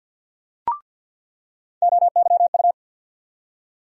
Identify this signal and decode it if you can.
Morse code, heard as a keyed tone, about 50 wpm, 700 Hz. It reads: OYU
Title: USA Call Sign Postfixes - 3 Letters 50wpm